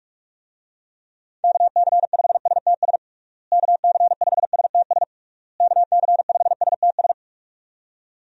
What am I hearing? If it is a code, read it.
Morse code, heard as a keyed tone, about 46 wpm, 700 Hz. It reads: KC5STS KC5STS KC5STS
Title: USA Call Signs - 6 Characters - Repeated 3x - 45wpm